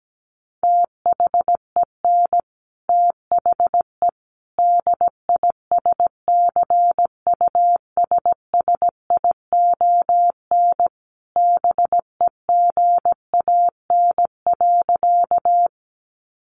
Morse code, heard as a keyed tone, about 17 wpm, 700 Hz. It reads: THEN THE DISCUSSION BEGAN.
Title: Sentences from Top 1100 Words 17wpm